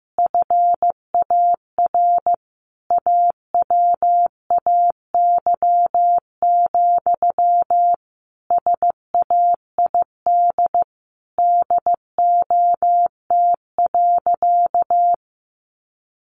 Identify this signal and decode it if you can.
Morse code, heard as a keyed tone, about 15 wpm, 700 Hz. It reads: FAR AWAY, SAID DOT.
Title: Sentences from Top 1500 Words 15wpm